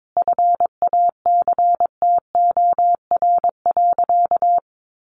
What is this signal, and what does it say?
Morse code, heard as a keyed tone, about 22 wpm, 700 Hz. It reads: FACTOR.
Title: Sentences from Top 1300 Words 22wpm